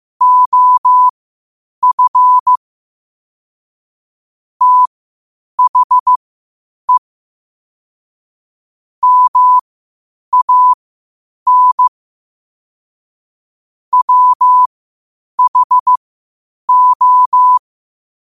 Morse code, heard as a keyed tone, about 15 wpm, spaced as for 8 wpm, 1000 Hz. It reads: OF THE MAN WHO
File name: qotdfiles / 8wpm